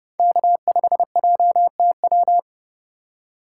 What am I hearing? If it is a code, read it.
Morse code, heard as a keyed tone, about 29 wpm, 700 Hz. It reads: K5JTW